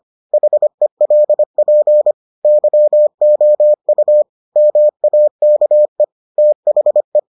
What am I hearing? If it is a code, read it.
Morse code, heard as a keyed tone, about 25 wpm, 600 Hz. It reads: HELP YOU MAKE THE